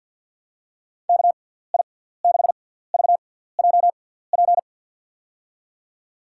Morse code, heard as a keyed tone, about 51 wpm, spaced as for 17 wpm, 700 Hz. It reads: KI6VJP
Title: USA Call Signs - 6 Characters - 6x Character Spacing - 50wpm